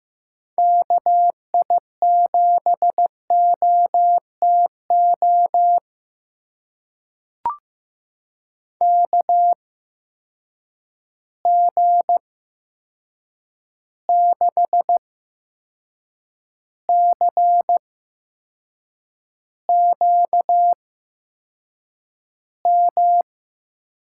Morse code, heard as a keyed tone, about 15 wpm, spaced as for 6 wpm, 700 Hz. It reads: KI7OTO KG6CQM
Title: USA Call Signs - 6 Characters - 8x Character Spacing - 15wpm